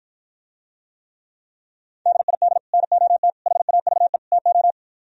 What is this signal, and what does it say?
Morse code, heard as a keyed tone, about 52 wpm, 700 Hz. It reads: DID NOT HAVE TO